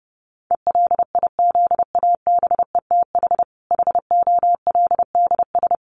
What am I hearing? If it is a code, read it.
Morse code, heard as a keyed tone, about 30 wpm, 700 Hz. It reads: ELIZABETH HOLDS
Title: Easy Phrases 30wpm